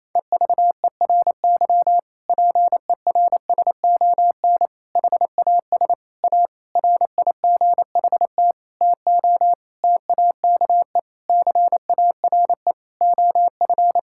Morse code, heard as a keyed tone, about 28 wpm, 700 Hz. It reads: EVERY PERSON HAS A RIGHT TO TAKE CARE OF
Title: Sentences from Top 1000 Words - Review Part 5 - 28wpm